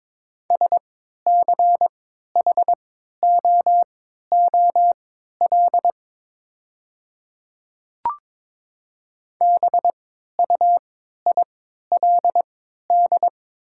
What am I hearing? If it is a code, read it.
Morse code, heard as a keyed tone, about 22 wpm, spaced as for 12 wpm, 700 Hz. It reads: SCHOOL BUILD
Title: 3x Character Spacing - Top 300 Words - 22wpm